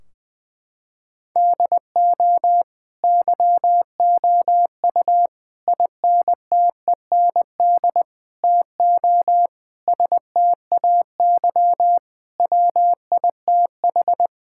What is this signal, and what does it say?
Morse code, heard as a keyed tone, about 21 wpm, 700 Hz. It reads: DO YOU INTEND TO STAY WITH
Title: Sentences from Top 1600 Words 20wpm